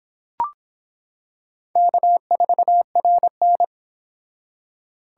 Morse code, heard as a keyed tone, about 24 wpm, 700 Hz. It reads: K4RN